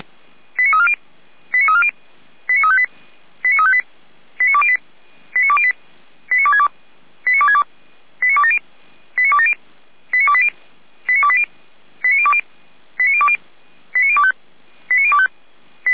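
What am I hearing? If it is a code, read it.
Five-tone selective call (ZVEI), heard as quick runs of five tones, tones of about 70 ms. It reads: 89340 89340 89358 89358 89208 89208 89262 89262 89270 89270 89280 89280 89020 89020 89025 89025
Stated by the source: ZVEI five-tone selective call received with an SDR.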